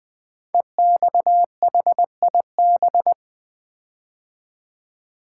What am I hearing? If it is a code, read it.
Morse code, heard as a keyed tone, about 20 wpm, 700 Hz. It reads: EXHIB
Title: Build A Word - 5 Letters At A Time - 10-Letter Words - 20wpm